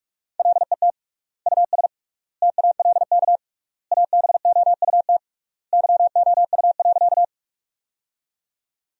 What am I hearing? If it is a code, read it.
Morse code, heard as a keyed tone, about 46 wpm, 700 Hz. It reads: LET US TALK ABOUT YOU.
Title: 3x Word-Spacing - Sentences from Top 400 Words - 45wpm